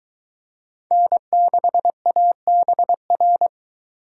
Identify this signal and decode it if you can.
Morse code, heard as a keyed tone, about 21 wpm, 700 Hz. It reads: N6ABR